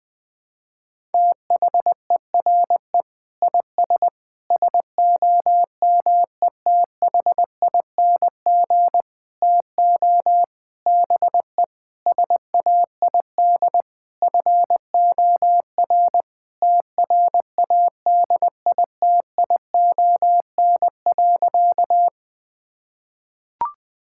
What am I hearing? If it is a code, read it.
Morse code, heard as a keyed tone, about 21 wpm, 700 Hz. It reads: THERE IS SOMETHING TO BE SAID FOR TRADITION.
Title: Sentences with 7-Word Phrases - 20wpm